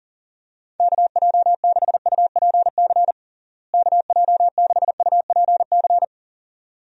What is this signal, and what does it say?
Morse code, heard as a keyed tone, about 39 wpm, 700 Hz. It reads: KJ6UPC KJ6UPC
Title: USA Call Signs - 6 Characters - Repeated 2x - 40wpm